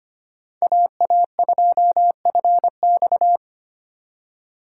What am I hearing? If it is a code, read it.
Morse code, heard as a keyed tone, about 25 wpm, 700 Hz. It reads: AA2FX
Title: USA Call Signs - 5 Characters - 2x Character Spacing - 25wpm